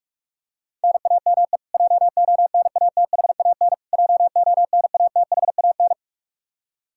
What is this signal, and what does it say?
Morse code, heard as a keyed tone, about 46 wpm, 700 Hz. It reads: NAME JONATHAN JONATHAN
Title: Practice QSO - Exchange 1 - Beginning - 45wpm